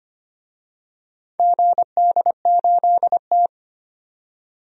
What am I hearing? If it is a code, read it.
Morse code, heard as a keyed tone, about 25 wpm, 690 Hz. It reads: GD8T
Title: DXCC-100 Call Signs - 4 Characters - 25wpm